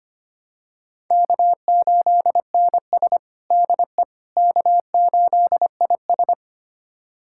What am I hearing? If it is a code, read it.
Morse code, heard as a keyed tone, about 25 wpm, 700 Hz. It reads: K8NS DE K8IS
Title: USA Callsign DE Callsign - 4 Characters - 25wpm